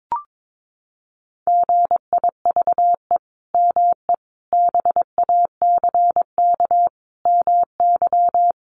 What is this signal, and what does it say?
Morse code, heard as a keyed tone, about 22 wpm, 700 Hz. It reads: GIVE ME BACK MY
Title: Sentences from Top 800 Words 22wpm